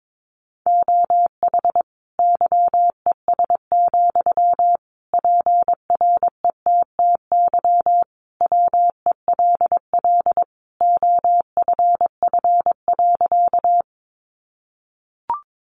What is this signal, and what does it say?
Morse code, heard as a keyed tone, about 22 wpm, 700 Hz. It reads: OH YES, PRETTY WELL OFF.
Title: Sentences from Top 1000 Words 22wpm